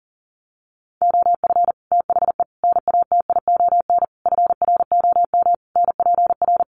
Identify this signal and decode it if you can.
Morse code, heard as a keyed tone, about 40 wpm, 700 Hz. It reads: OF THE NATION FROM NPR
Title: Sets of 5 Words 40wpm